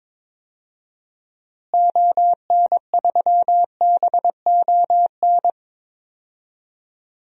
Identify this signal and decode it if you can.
Morse code, heard as a keyed tone, about 22 wpm, 700 Hz. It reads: ON3BON
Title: DXCC-100 Call Signs - 6 Characters - Repeated 2x - 22wpm